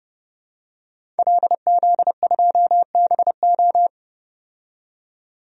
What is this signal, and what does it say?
Morse code, heard as a keyed tone, about 30 wpm, 700 Hz. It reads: LZ2BO
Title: DXCC-100 Call Signs - 5 Characters - 30wpm